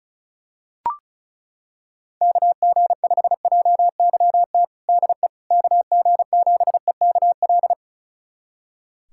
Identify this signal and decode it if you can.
Morse code, heard as a keyed tone, about 36 wpm, 690 Hz. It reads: KG5JYT DE KG7EKL
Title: USA Callsign DE Callsign 35wpm